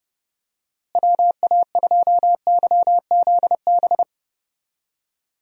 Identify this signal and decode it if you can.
Morse code, heard as a keyed tone, about 32 wpm, 700 Hz. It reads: WA2YZB
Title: USA Call Signs - 6 Characters - Repeated 3x - 30wpm